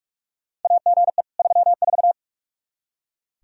Morse code, heard as a keyed tone, about 42 wpm, 700 Hz. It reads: AGE 34
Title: Practice QSO - Exchange 3 - More Info - 45wpm